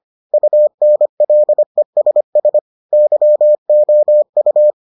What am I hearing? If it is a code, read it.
Morse code, heard as a keyed tone, about 25 wpm, 600 Hz. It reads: UNLESS YOU